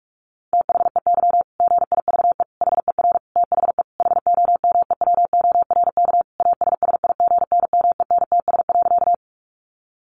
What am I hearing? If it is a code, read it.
Morse code, heard as a keyed tone, about 44 wpm, 700 Hz. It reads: THEY GIVE HER THE HOMEWORK ASSIGNMENTS.
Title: Easy Phrases 45wpm